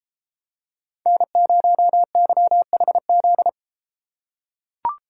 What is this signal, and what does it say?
Morse code, heard as a keyed tone, about 31 wpm, 700 Hz. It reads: N0YHZ